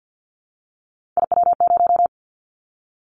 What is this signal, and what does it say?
Morse code, heard as a keyed tone, about 50 wpm, 700 Hz. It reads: IW0
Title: Callsign Prefixes for Top 25 DXCC - 50wpm